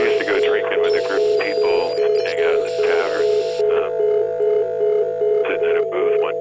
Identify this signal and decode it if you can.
Congestion tone (425 Hz), with speech.